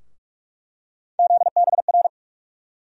Morse code, heard as a keyed tone, about 46 wpm, 700 Hz. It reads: ZBR
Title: USA Call Sign Postfixes - 3 Letters 45wpm